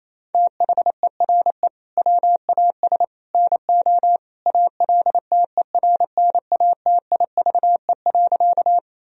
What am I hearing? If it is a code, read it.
Morse code, heard as a keyed tone, about 28 wpm, 700 Hz. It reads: THERE WAS NO ALTERNATIVE.
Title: Sentences from Top 1000 Words - Review Part 4 - 28wpm